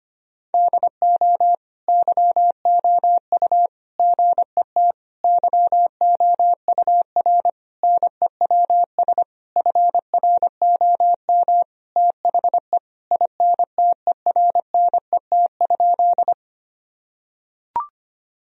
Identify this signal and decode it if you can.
Morse code, heard as a keyed tone, about 25 wpm, 700 Hz. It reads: DO YOU GET YOUR NEWS FROM THE INTERNET?
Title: Sentences from Top 1000 Words - Review Part 4 - 25wpm